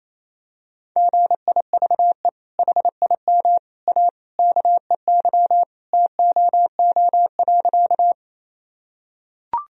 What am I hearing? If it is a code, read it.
Morse code, heard as a keyed tone, about 28 wpm, 700 Hz. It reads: GIVE HIM A KEY TOO.
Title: Sentences from Top 1000 Words - Review Part 4 - 28wpm